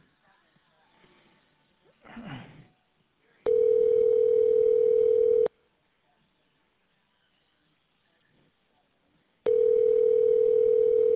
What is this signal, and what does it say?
Ringback tone (North America), 440 + 480 Hz.